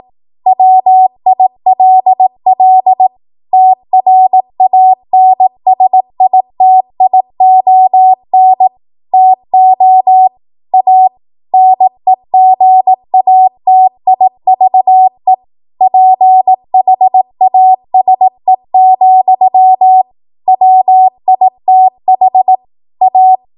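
Morse code, heard as a keyed tone, about 18 wpm, 750 Hz. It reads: WILL TRANSITION TO A NEGATIVE PHASE, WITH A
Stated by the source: 18 WPM Morse code practice from a national amateur radio association's code archive.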